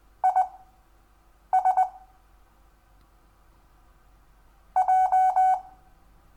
Morse code, heard as a keyed tone, about 20 wpm, spaced as for 7 wpm, 750 Hz. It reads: IS J
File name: cw / Lesson10 REVIEW